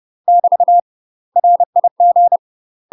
Morse code, heard as a keyed tone, about 30 wpm, 700 Hz. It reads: <BT> RIG
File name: UK MASTER QSOs TRAINING / 30WPM / UK MASTER QSOs 0035